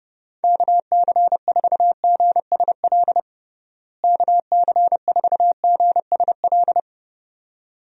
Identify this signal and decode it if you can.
Morse code, heard as a keyed tone, about 32 wpm, 700 Hz. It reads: KC4GSL KC4GSL
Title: USA Call Signs - 6 Characters - Repeated 2x - 30wpm